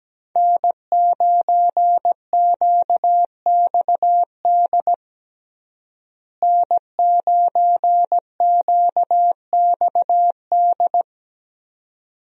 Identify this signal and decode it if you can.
Morse code, heard as a keyed tone, about 17 wpm, 700 Hz. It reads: N9QXD N9QXD
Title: USA Call Signs - 5 Characters - Repeated 2x - 17wpm